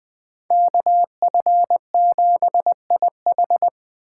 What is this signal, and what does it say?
Morse code, heard as a keyed tone, about 24 wpm, 700 Hz. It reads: KF7IH
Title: Speed Racing - USA 2x2 Call Signs 20wpm